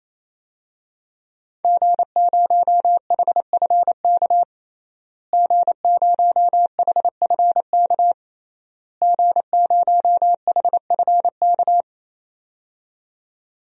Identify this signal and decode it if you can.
Morse code, heard as a keyed tone, about 28 wpm, 700 Hz. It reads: G0HFK G0HFK G0HFK
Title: DXCC-100 Call Signs - 5 Characters - Repeated 3x - 28wpm